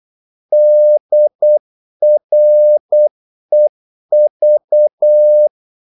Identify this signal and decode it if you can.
Morse code, heard as a keyed tone, about 8 wpm, 600 Hz. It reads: DREV
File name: Korte ord DK 8wpm